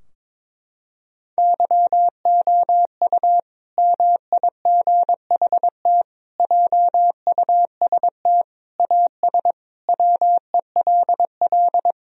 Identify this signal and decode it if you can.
Morse code, heard as a keyed tone, about 22 wpm, 700 Hz. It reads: YOU MIGHT JUST AS WELL
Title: Sentences from Top 1000 Words - Review Part 2 - 22wpm